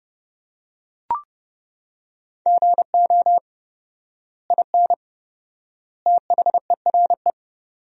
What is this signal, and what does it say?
Morse code, heard as a keyed tone, about 30 wpm, 700 Hz. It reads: GO IN THERE
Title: 4x Word-Spacing - Sentences from Top 500 Words - 30wpm